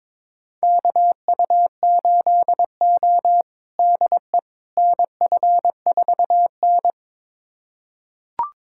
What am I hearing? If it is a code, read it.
Morse code, heard as a keyed tone, about 22 wpm, 700 Hz. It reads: KU8O DE NF4N
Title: USA Callsign DE Callsign - 4 Characters - 22wpm